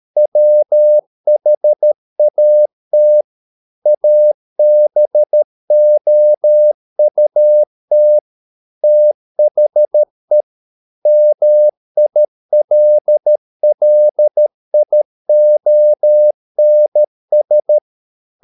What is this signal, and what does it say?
Morse code, heard as a keyed tone, about 13 wpm, 600 Hz. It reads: WHAT ABOUT THE MILLIONS